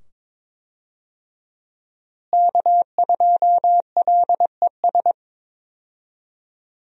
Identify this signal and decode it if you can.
Morse code, heard as a keyed tone, about 22 wpm, 700 Hz. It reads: K2LES
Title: USA Call Signs - 5 Characters - Repeated 2x - 22wpm